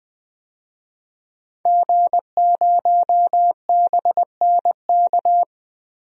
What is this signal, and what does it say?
Morse code, heard as a keyed tone, about 20 wpm, 700 Hz. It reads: G0BNK
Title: DXCC-100 Call Signs - 5 Characters - Repeated 2x - 20wpm